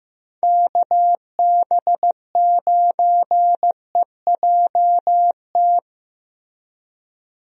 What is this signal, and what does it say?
Morse code, heard as a keyed tone, about 15 wpm, 700 Hz. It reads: KB9EJT
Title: USA Call Signs - 6 Characters - Repeated 3x - 15wpm